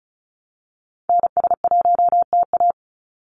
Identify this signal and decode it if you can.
Morse code, heard as a keyed tone, about 34 wpm, 700 Hz. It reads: NS1TA
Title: NA SOTA Activator Callsigns 35wpm